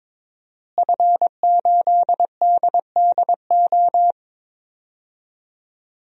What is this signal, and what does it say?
Morse code, heard as a keyed tone, about 22 wpm, 700 Hz. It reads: F8DDO
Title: DXCC-100 Call Signs - 5 Characters - Repeated 3x - 22wpm